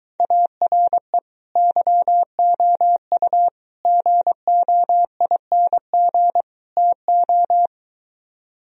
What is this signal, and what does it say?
Morse code, heard as a keyed tone, about 22 wpm, 700 Hz. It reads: ARE YOU GOING TO